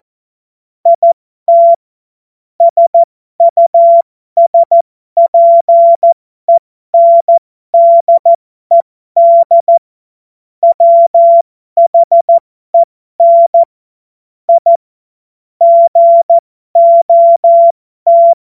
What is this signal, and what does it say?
Morse code, heard as a keyed tone, about 14 wpm, spaced as for 12 wpm, 680 Hz. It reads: IT SUSPENDED WHEN I GOT
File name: Morsekurs / Witze